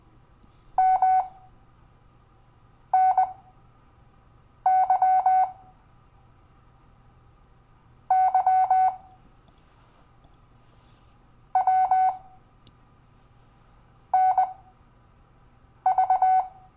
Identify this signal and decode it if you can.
Morse code, heard as a keyed tone, about 20 wpm, spaced as for 5 wpm, 750 Hz. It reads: MNY Y W NV
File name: cw / Lesson12 YV